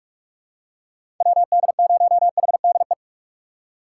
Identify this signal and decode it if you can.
Morse code, heard as a keyed tone, about 42 wpm, 700 Hz. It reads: WD0HDE
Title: USA Call Signs - 6 Characters - 2x Character Spacing - 45wpm